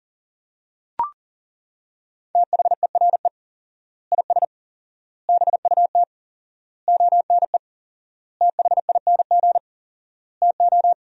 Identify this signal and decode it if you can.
Morse code, heard as a keyed tone, about 40 wpm, 700 Hz. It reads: THERE IS BUT ONE THING TO
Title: 4x Word-Spacing - Sentences from Top 100 Words - 40wpm